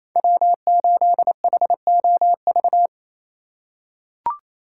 Morse code, heard as a keyed tone, about 28 wpm, 700 Hz. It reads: W8HOV
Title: USA Call Signs - 5 Characters - 2x Character Spacing - 28wpm